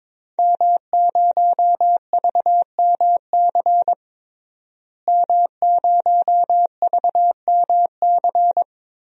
Morse code, heard as a keyed tone, about 22 wpm, 700 Hz. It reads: M0VMC M0VMC
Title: DXCC-100 Call Signs - 5 Characters - Repeated 2x - 22wpm